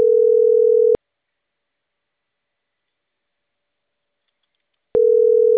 Ringback tone (North America), 440 + 480 Hz.
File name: CENTRAL CAEA / out agcaea01 - .72275